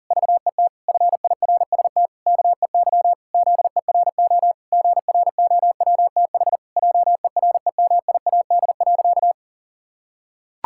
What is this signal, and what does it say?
Morse code, heard as a keyed tone, about 37 wpm, 700 Hz. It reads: VET FIRST KEY ZERO GROWTH JEREMIAD.
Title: Pangram Challenge v2 - 40wpm